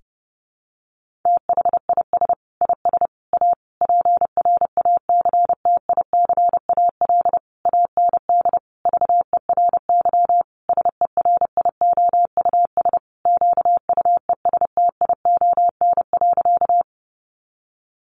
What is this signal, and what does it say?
Morse code, heard as a keyed tone, about 30 wpm, 700 Hz. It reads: THIS IS A PRACTICAL AND VERY SERIOUS QUESTION.
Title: Sentences from Top 1400 Words 30wpm